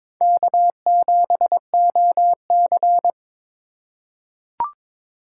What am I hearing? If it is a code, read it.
Morse code, heard as a keyed tone, about 21 wpm, 700 Hz. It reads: K7OC
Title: USA Call Signs - 4 Characters - 2x Character Spacing - 22wpm